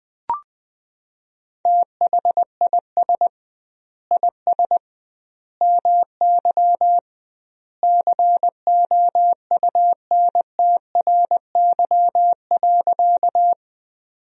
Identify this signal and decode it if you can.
Morse code, heard as a keyed tone, about 20 wpm, 700 Hz. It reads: THIS IS MY COUNTRY.
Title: 2x Word-Spacing - Sentences from Top 300 Words - 20wpm